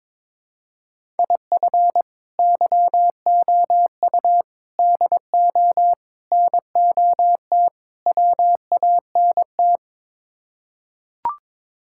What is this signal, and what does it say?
Morse code, heard as a keyed tone, about 21 wpm, 700 Hz. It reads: IF YOU DO NOT WANT
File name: practice / 22wpm / Build A Phrase - 5-Word Phrases - 22wpm